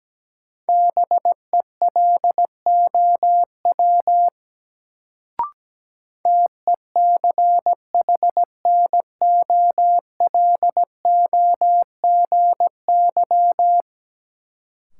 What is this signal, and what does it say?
Morse code, heard as a keyed tone, about 17 wpm, 700 Hz. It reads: BELOW TECHNOLOGY
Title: Sentences from Top 1000 Words - Review Part 5 - New Word Drill - 17wpm